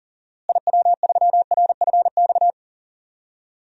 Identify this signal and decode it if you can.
Morse code, heard as a keyed tone, about 37 wpm, 700 Hz. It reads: IW3RFX
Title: DXCC-100 Call Signs - 6 Characters - Repeated 2x - 40wpm